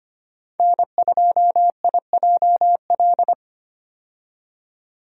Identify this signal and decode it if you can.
Morse code, heard as a keyed tone, about 25 wpm, 700 Hz. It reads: N2IJL
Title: USA Call Signs - 5 Characters - 25wpm